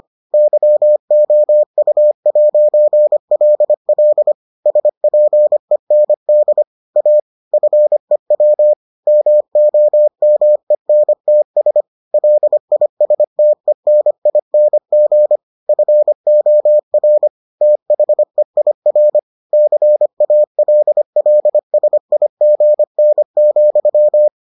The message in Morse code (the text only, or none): YOU'LL SPEND A FEW MOMENTS LISTENING FOR THEIR CALLSIGN,